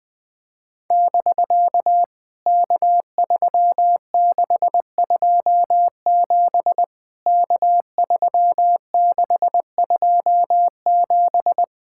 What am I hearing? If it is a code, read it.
Morse code, heard as a keyed tone, about 20 wpm, 700 Hz. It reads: <BK> K3627 K3627
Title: POTA QSO Practice - 20wpm